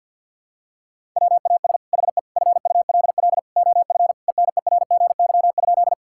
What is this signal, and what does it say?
Morse code, heard as a keyed tone, about 50 wpm, 700 Hz. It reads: WAS HE FULL OF ENERGY?
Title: Sentences from Top 1000 Words - Review Part 6 - 50wpm